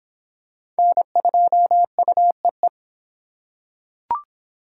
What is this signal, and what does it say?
Morse code, heard as a keyed tone, about 23 wpm, 700 Hz. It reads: N2UEE